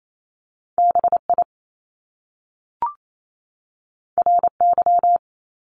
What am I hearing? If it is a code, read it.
Morse code, heard as a keyed tone, about 28 wpm, 700 Hz. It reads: B I R Y
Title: Top Two Letter Combinations in Words 28wpm